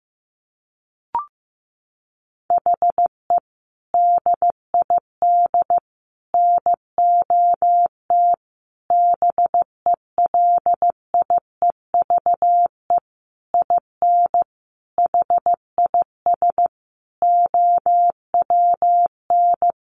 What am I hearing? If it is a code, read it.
Morse code, heard as a keyed tone, about 15 wpm, 700 Hz. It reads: HE DID NOT BELIEVE IN HIS OWN
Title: Sentences from Top 1300 Words 15wpm